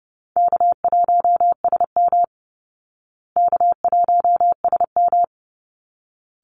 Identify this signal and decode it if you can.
Morse code, heard as a keyed tone, about 30 wpm, 700 Hz. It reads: K1SM K1SM
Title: Call Signs Repeated 3x - 30wpm